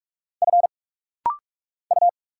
Morse code, heard as a keyed tone, about 45 wpm, 700 Hz. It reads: F U
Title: Single Letter-Number Alt - Rapid-Fire 45wpm